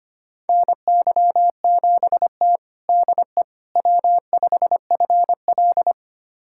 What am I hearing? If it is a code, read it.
Morse code, heard as a keyed tone, about 25 wpm, 700 Hz. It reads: NY7T DE W5FL
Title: USA Callsign DE Callsign - 4 Characters - 25wpm